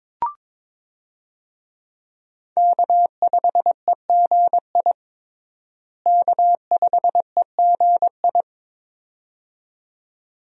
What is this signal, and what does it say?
Morse code, heard as a keyed tone, about 22 wpm, 700 Hz. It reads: K5EGI K5EGI
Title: USA Call Signs - 5 Characters - Repeated 2x - 22wpm